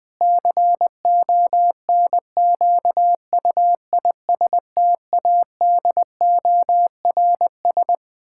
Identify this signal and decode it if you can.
Morse code, heard as a keyed tone, about 20 wpm, 700 Hz. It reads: CONQUISTADORS
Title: Pangram Challenge v2 - 20wpm